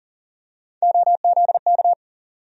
Morse code, heard as a keyed tone, about 40 wpm, 700 Hz. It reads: OZK
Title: USA Call Sign Postfixes - 3 Letters 40wpm